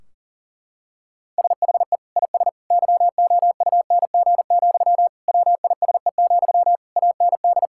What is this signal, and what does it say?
Morse code, heard as a keyed tone, about 40 wpm, 700 Hz. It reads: SHE IS YOUNG, WISE, AND
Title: Sentences from Top 1600 Words 40wpm